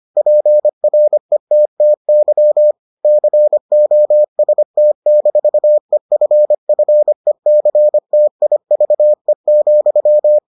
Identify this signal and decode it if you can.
Morse code, heard as a keyed tone, about 25 wpm, 600 Hz. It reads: PRETTY COST-EFFECTIVE,